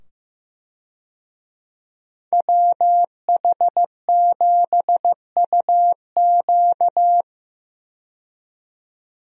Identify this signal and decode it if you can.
Morse code, heard as a keyed tone, about 15 wpm, 700 Hz. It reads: WH7UQ
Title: USA Call Signs - 5 Characters - 15wpm